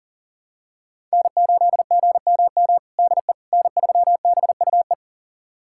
Morse code, heard as a keyed tone, about 40 wpm, 700 Hz. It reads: N8GMM DE N3BUE
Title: USA Callsign DE Callsign - 5 Characters - 40wpm